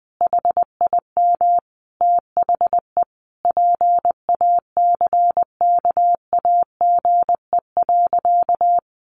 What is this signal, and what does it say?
Morse code, heard as a keyed tone, about 20 wpm, 700 Hz. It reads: HIM THE PACKAGE.
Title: Easy Phrases 20wpm